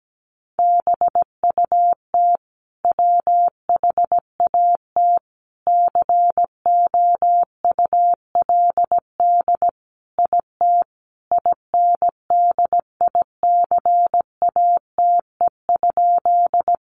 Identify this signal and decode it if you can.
Morse code, heard as a keyed tone, about 17 wpm, 700 Hz. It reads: BUT WHAT COULD IT INDICATE?
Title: Sentences from Top 1300 Words 17wpm